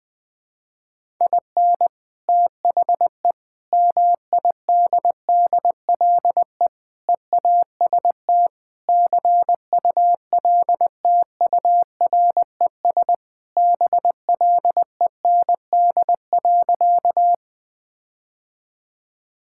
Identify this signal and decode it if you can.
Morse code, heard as a keyed tone, about 20 wpm, 690 Hz. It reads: IN THE MIDDLE EAST CULTURES BLEND.
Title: Sentences with 4-Word Phrases - Spoken Beginning - 20wpm